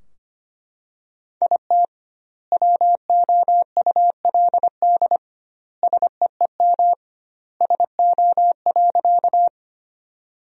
Morse code, heard as a keyed tone, about 25 wpm, 700 Hz. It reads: IT WOULD SEEM SO.
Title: 2x Word-Spacing - Sentences from Top 300 Words - 25wpm